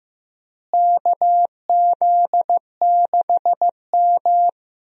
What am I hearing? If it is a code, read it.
Morse code, heard as a keyed tone, about 15 wpm, 700 Hz. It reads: KZ6M
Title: USA Call Signs - 4 Characters - Repeated 3x - 15wpm